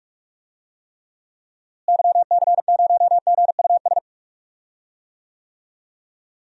Morse code, heard as a keyed tone, about 46 wpm, 700 Hz. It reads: YC0GUS
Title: DXCC-100 Call Signs - 6 Characters - Repeated 3x - 45wpm